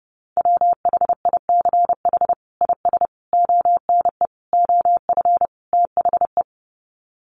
Morse code, heard as a keyed tone, about 30 wpm, 700 Hz. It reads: WHICH IS ONE OF THE
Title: Sets of 5 Words Encore 30wpm